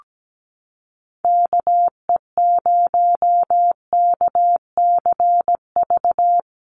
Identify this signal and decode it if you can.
Morse code, heard as a keyed tone, about 17 wpm, 700 Hz. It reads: KE0KCV
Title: USA Call Signs Repeated 3x 17wpm